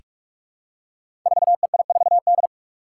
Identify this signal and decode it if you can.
Morse code, heard as a keyed tone, about 44 wpm, 700 Hz. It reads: 4EI4D
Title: Five Challenging Letter-Number Combinations 45wpm